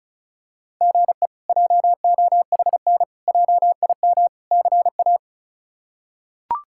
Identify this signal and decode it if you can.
Morse code, heard as a keyed tone, about 34 wpm, 700 Hz. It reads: GE JOHN JIM CA
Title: K1USN SST QSO Practice - 35wpm